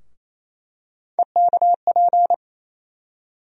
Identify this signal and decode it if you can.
Morse code, heard as a keyed tone, about 28 wpm, 700 Hz. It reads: EKP
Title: USA Call Sign Postfixes - 3 Letters 28wpm